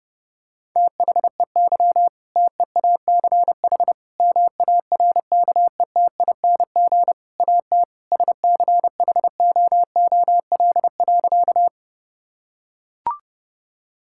Morse code, heard as a keyed tone, about 30 wpm, 700 Hz. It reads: THEY TEACH MARKETING AT SCHOOL.
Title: Sentences from Top 1000 Words - Review Part 5 - 30wpm